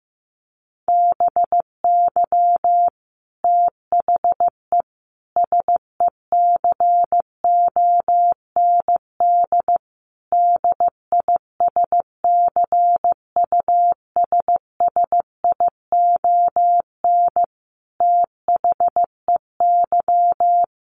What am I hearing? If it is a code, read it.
Morse code, heard as a keyed tone, about 15 wpm, 700 Hz. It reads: BY THE SECOND DISCUSSION THEY
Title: Sentences from Top 1200 Words 15wpm